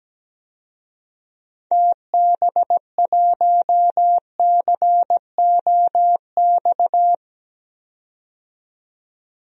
Morse code, heard as a keyed tone, about 17 wpm, 700 Hz. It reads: TB1COX
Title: DXCC-100 Call Signs - 6 Characters - 17wpm